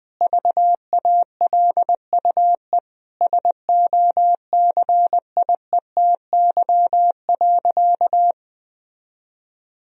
Morse code, heard as a keyed tone, about 20 wpm, 700 Hz. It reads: VALUE SOCIETY.
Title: Sentences from Top 1000 Words - Review Part 4 - 20wpm